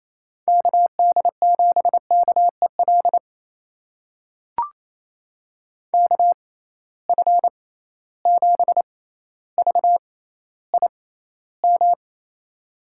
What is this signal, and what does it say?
Morse code, heard as a keyed tone, about 28 wpm, spaced as for 14 wpm, 700 Hz. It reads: KD7KEL KF7VIM
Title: USA Call Signs - 6 Characters - 6x Character Spacing - 28wpm